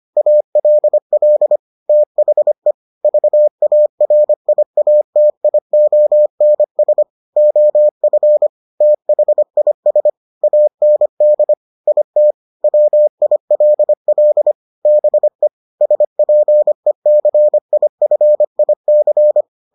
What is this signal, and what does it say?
Morse code, heard as a keyed tone, about 25 wpm, 600 Hz. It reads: ALL THE VARIATIONS OF THIS AND IT WILL BE SPECIFIC